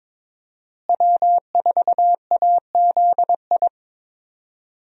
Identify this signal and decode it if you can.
Morse code, heard as a keyed tone, about 21 wpm, 700 Hz. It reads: W4AZI
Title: USA Call Signs - 5 Characters - Repeated 3x - 22wpm